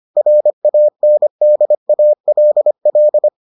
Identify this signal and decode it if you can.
Morse code, heard as a keyed tone, about 25 wpm, 600 Hz. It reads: RANDALL